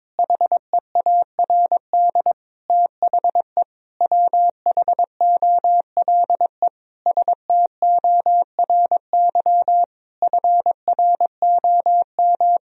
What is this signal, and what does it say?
Morse code, heard as a keyed tone, about 22 wpm, 700 Hz. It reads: HEARD THE WHOLE STORY FROM
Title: Sentences from Top 1000 Words - Review Part 4 - 22wpm